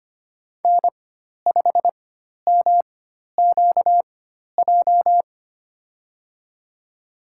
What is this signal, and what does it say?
Morse code, heard as a keyed tone, about 25 wpm, spaced as for 12 wpm, 700 Hz. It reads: N5MQJ
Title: USA Call Signs - 4x Character Spacing - 25wpm